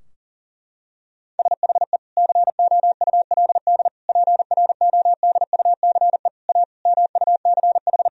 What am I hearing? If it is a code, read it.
Morse code, heard as a keyed tone, about 40 wpm, 700 Hz. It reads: SHE COULD PRODUCE A MUCH
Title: Sentences from Top 1000 Words - Review Part 3 - 40wpm